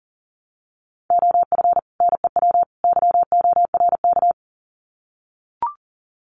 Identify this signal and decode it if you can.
Morse code, heard as a keyed tone, about 40 wpm, 700 Hz. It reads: OF NEW YORK
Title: Sets of 3 Words 40wpm